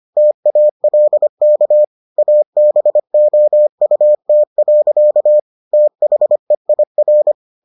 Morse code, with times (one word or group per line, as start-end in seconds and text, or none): 0.2-1.8 TALK
2.2-5.4 ABOUT.
5.7-7.3 THEIR